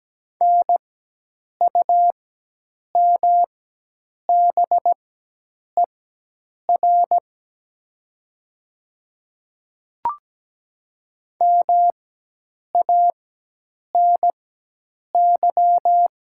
Morse code, heard as a keyed tone, about 17 wpm, spaced as for 8 wpm, 700 Hz. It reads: NUMBER MANY
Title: 4x Character Spacing - Top 100 Words - 17wpm